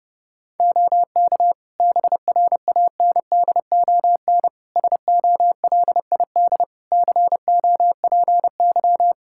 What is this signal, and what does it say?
Morse code, heard as a keyed tone, about 32 wpm, 700 Hz. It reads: OK BRANDON SOLID COPY
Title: K1USN SST QSO Practice 30wpm